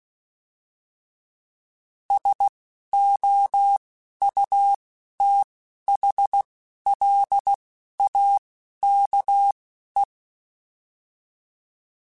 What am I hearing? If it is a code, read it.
Morse code, heard as a keyed tone, about 16 wpm, spaced as for 11 wpm, 800 Hz. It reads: SOUTHLAKE